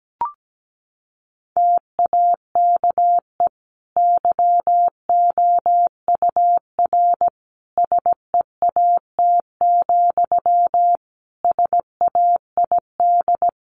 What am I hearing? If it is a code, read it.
Morse code, heard as a keyed tone, about 17 wpm, 700 Hz. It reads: TAKE YOUR SEAT, SAID
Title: Sentences from Top 1500 Words 17wpm